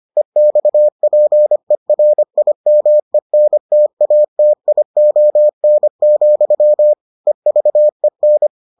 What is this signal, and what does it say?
Morse code, heard as a keyed tone, about 25 wpm, 600 Hz. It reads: EXPERIMENTATION, EVEN